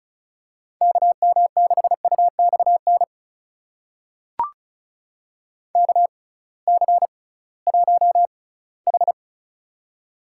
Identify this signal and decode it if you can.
Morse code, heard as a keyed tone, about 36 wpm, spaced as for 20 wpm, 700 Hz. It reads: KM6UXN KC1H
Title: USA Call Signs - 6x Character Spacing - 35wpm